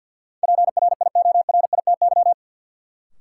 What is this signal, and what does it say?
Morse code, heard as a keyed tone, about 53 wpm, 700 Hz. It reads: PRIORITY
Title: Sentences from Top 1600 Words - New Word Drill - 50wpm